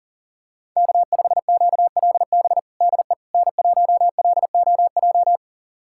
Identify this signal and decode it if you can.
Morse code, heard as a keyed tone, about 37 wpm, 700 Hz. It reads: K5QLB DE N1LOJ
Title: USA Callsign DE Callsign - 5 Characters - 40wpm